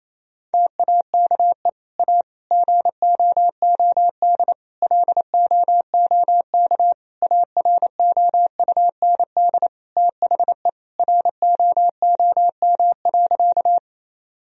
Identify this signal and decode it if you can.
Morse code, heard as a keyed tone, about 28 wpm, 700 Hz. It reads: TAKE A GOOD LOOK AROUND THE ROOM.
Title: Sentences from Top 1000 Words - Review Part 3 - 28wpm